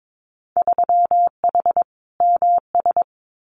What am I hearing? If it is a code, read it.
Morse code, heard as a keyed tone, about 22 wpm, 700 Hz. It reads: 3H MS
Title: Winter Field Day Exchanges Encore 22wpm